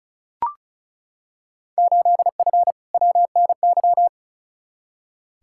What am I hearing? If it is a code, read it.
Morse code, heard as a keyed tone, about 36 wpm, 700 Hz. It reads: 8F WNY
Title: Field Day Exchanges Encore - 35wpm